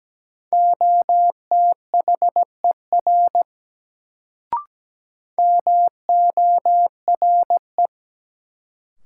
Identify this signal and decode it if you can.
Morse code, heard as a keyed tone, about 17 wpm, 700 Hz. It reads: OTHER MORE
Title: Sentences from Top 100 Words - New Word Drill - 17wpm